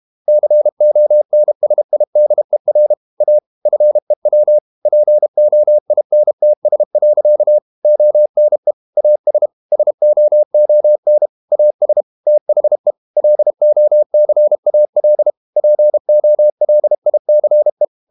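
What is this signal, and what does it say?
Morse code, heard as a keyed tone, about 32 wpm, 600 Hz. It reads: CONSIDER A FEW POINTS. ONE AS SOON AS THE LOCAL POLICE
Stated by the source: Morse code practice at 32 wpm from CW med Gnister, reading Ken Follett's novel Never in English.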